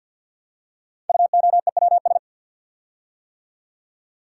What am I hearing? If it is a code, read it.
Morse code, heard as a keyed tone, about 50 wpm, 700 Hz. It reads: UOEWS
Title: Code Groups - 5 Letters - 50wpm